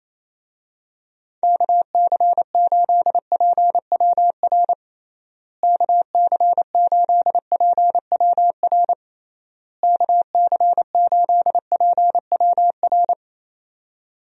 Morse code, heard as a keyed tone, about 28 wpm, 700 Hz. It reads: KC8PWR KC8PWR KC8PWR
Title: USA Call Signs - 6 Characters - Repeated 3x - 28wpm